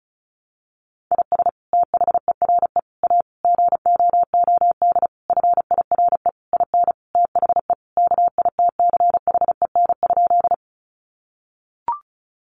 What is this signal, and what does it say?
Morse code, heard as a keyed tone, about 34 wpm, 700 Hz. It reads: IS THERE A GOOD FIRE IN THE KITCHEN?
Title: Sentences from Top 1400 Words 35wpm